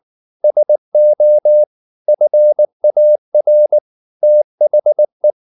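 Morse code, heard as a keyed tone, about 19 wpm, 600 Hz. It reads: SO FAR THE
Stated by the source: Morse code at 19 wpm from CW med Gnister, reading Ken Follett's novel Never in English.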